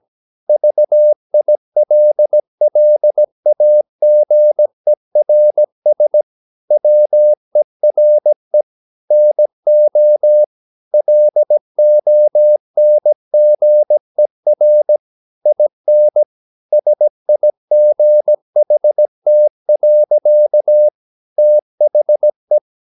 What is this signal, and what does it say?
Morse code, heard as a keyed tone, about 17 wpm, 600 Hz. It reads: VILLAGERS WERE NO LONGER IN SIGHT. THE
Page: Romanen Never af Ken Follet på engelsk.